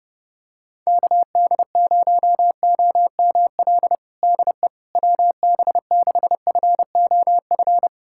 Morse code, heard as a keyed tone, about 30 wpm, 700 Hz. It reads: KD0OML DE WB6FOF
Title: USA Callsign DE Callsign 30wpm